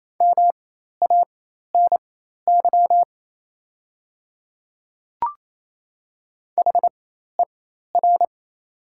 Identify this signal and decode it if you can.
Morse code, heard as a keyed tone, about 27 wpm, spaced as for 13 wpm, 690 Hz. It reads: MANY HER
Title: 4x Character Spacing - Top 100 Words - 28wpm